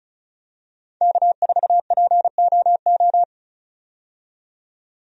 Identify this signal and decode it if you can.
Morse code, heard as a keyed tone, about 36 wpm, 700 Hz. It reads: K4POO
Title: USA Call Signs - 5 Characters - 35wpm